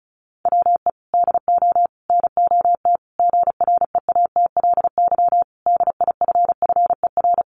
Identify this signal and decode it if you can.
Morse code, heard as a keyed tone, about 34 wpm, 700 Hz. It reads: WE DO NOT GREATLY DIFFER
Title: Sentences from Top 1400 Words 35wpm